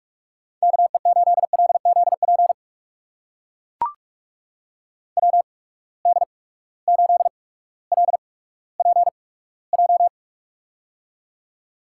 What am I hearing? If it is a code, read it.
Morse code, heard as a keyed tone, about 42 wpm, spaced as for 19 wpm, 700 Hz. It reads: KE8LZP WD8LPJ
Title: USA Call Signs - 6 Characters - 8x Character Spacing - 45wpm